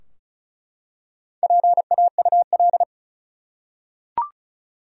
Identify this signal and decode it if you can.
Morse code, heard as a keyed tone, about 34 wpm, 700 Hz. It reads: PAUL
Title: USA Common Names - Max 4 Letters - 35wpm